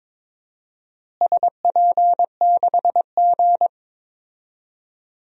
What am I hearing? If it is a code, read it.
Morse code, heard as a keyed tone, about 22 wpm, 700 Hz. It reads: SP6G
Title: DXCC-100 Call Signs - 4 Characters - Repeated 2x - 22wpm